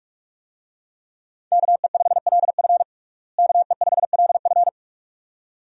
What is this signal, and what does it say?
Morse code, heard as a keyed tone, about 47 wpm, 700 Hz. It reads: KE5LF KE5LF
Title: USA Call Signs - 5 Characters - Repeated 2x - 45wpm